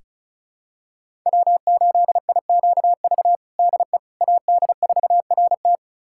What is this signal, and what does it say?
Morse code, heard as a keyed tone, about 36 wpm, 700 Hz. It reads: W8IQV DE AD4RT
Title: USA Callsign DE Callsign 35wpm